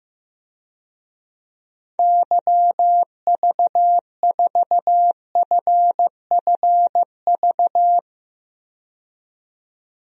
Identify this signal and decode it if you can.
Morse code, heard as a keyed tone, about 15 wpm, 700 Hz. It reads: YV4FFV
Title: DXCC-100 Call Signs - 6 Characters - 15wpm